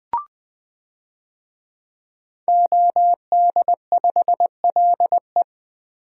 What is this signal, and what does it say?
Morse code, heard as a keyed tone, about 21 wpm, 700 Hz. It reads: OD5LE
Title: DXCC-100 Call Signs - 5 Characters - Repeated 2x - 20wpm